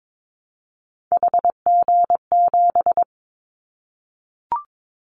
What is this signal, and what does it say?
Morse code, heard as a keyed tone, about 22 wpm, 700 Hz. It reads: HG7
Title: Callsign Prefixes for Top 25 DXCC - 22wpm